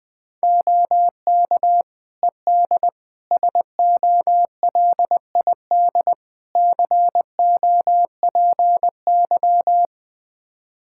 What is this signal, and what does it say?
Morse code, heard as a keyed tone, about 20 wpm, 700 Hz. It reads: OK ED SOLID COPY
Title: K1USN SST QSO Practice 20wpm